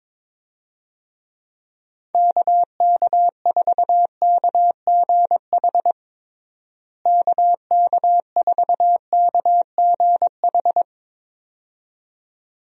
Morse code, heard as a keyed tone, about 22 wpm, 700 Hz. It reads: KK4KGH KK4KGH
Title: USA Call Signs - 6 Characters - Repeated 2x - 22wpm